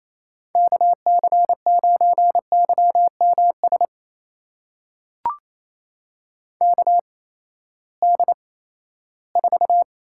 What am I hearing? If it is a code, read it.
Morse code, heard as a keyed tone, about 28 wpm, spaced as for 16 wpm, 700 Hz. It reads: KC9YMS KD4
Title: USA Call Signs - 8x Character Spacing - 28wpm